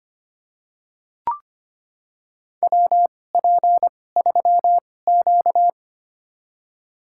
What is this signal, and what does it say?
Morse code, heard as a keyed tone, about 25 wpm, spaced as for 18 wpm, 700 Hz. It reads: WP3Q
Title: USA Call Signs - 4 Characters - 2x Character Spacing - 25wpm